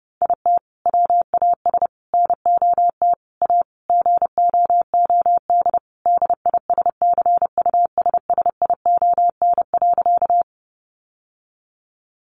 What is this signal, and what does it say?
Morse code, heard as a keyed tone, about 28 wpm, 700 Hz. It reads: IT WAS NOT A GOOD DISCUSSION.
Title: Sentences from Top 1100 Words 30wpm